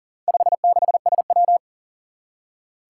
Morse code, heard as a keyed tone, about 39 wpm, 700 Hz. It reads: 56SW
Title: Four Challenging Letter-Number Combinations 40wpm